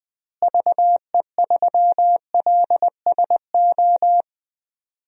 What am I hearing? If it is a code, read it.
Morse code, heard as a keyed tone, about 20 wpm, 700 Hz. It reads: VE3LSO
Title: DXCC-100 Call Signs - 6 Characters - Repeated 2x - 20wpm